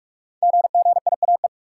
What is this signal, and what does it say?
Morse code, heard as a keyed tone, about 44 wpm, 700 Hz. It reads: GGIAE
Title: Code Groups - 5 Letters - 45wpm